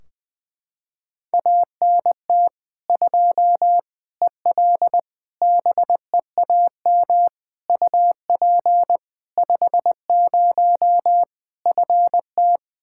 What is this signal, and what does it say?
Morse code, heard as a keyed tone, about 20 wpm, 700 Hz. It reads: ANT 2 EL BEAM UP 50 FT
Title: K1USN SST QSO Practice 20wpm